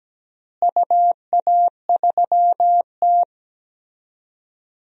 Morse code, heard as a keyed tone, about 17 wpm, 700 Hz. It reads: UA3T
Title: DXCC-100 Call Signs - 4 Characters - Repeated 3x - 17wpm